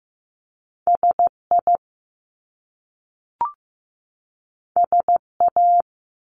Morse code, heard as a keyed tone, about 15 wpm, 700 Hz. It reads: SI SA
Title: Top Two Letter Combinations in Words 15wpm